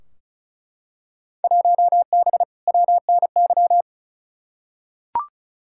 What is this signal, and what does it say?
Morse code, heard as a keyed tone, about 34 wpm, 700 Hz. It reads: 1B WNY
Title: Field Day Exchanges - 35wpm